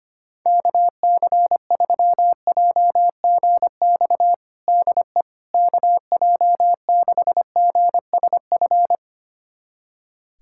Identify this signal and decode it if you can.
Morse code, heard as a keyed tone, about 25 wpm, 700 Hz. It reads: KC3JGX DE KJ6GSF
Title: USA Callsign DE Callsign - 6 Characters - 25wpm